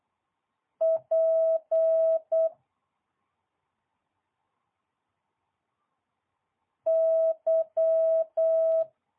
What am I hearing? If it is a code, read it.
Morse code, heard as a keyed tone, about 8 wpm, 650 Hz. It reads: P Y
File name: jh8cbh / 2022cw